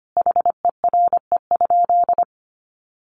Morse code, heard as a keyed tone, about 25 wpm, 700 Hz. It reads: HERE?
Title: Sentences from Top 500 Words 25wpm